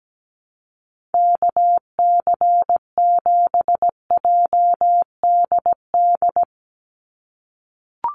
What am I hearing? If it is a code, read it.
Morse code, heard as a keyed tone, about 17 wpm, 700 Hz. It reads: KC7JDD
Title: USA Call Signs 17wpm